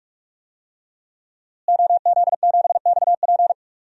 Morse code, heard as a keyed tone, about 46 wpm, 700 Hz. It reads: OZ7XP
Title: DXCC-100 Call Signs - 5 Characters - Repeated 2x - 45wpm